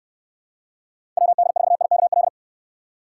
Morse code, heard as a keyed tone, about 69 wpm, 700 Hz. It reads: WD4IKL